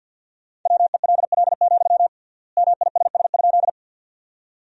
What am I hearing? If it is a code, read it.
Morse code, heard as a keyed tone, about 53 wpm, 700 Hz. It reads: WELL, MISS?
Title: 3x Word-Spacing - Sentences from Top 500 Words - 50wpm